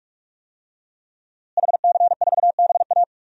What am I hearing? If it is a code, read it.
Morse code, heard as a keyed tone, about 43 wpm, 700 Hz. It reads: HC4BA
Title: DXCC-100 Call Signs - 5 Characters - Repeated 3x - 45wpm